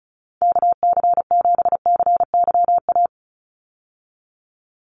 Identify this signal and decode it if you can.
Morse code, heard as a keyed tone, about 34 wpm, 700 Hz. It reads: KC7CYA
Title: USA Call Signs 35wpm